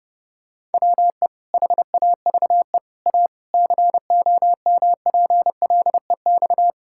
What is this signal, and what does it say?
Morse code, heard as a keyed tone, about 32 wpm, 700 Hz. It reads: WE HAVE A COMPLEX
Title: Sentences from Top 1000 Words - Review Part 6 - 30wpm